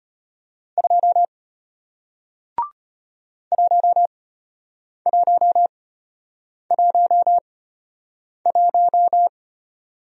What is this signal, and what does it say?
Morse code, heard as a keyed tone, about 33 wpm, spaced as for 8 wpm, 700 Hz. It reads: 2 1111